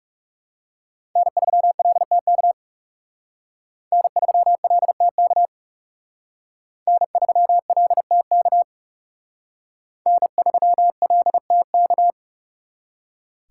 Morse code, heard as a keyed tone, about 36 wpm, 700 Hz. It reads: N3LTK N3LTK N3LTK N3LTK
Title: Speed Racing - USA 1x3 Call Signs 30wpm